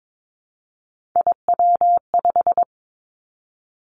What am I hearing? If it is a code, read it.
Morse code, heard as a keyed tone, about 22 wpm, 700 Hz. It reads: IW5
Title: Three Challenging Letter-Number Combinations 22wpm